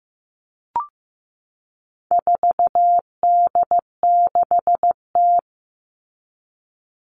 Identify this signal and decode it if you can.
Morse code, heard as a keyed tone, about 15 wpm, 700 Hz. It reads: 4D6T
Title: All Cut Numbers - 4-Digit 15wpm